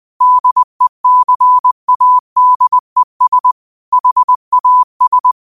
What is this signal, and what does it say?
Morse code, heard as a keyed tone, about 20 wpm, 1000 Hz. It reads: DECADES HAS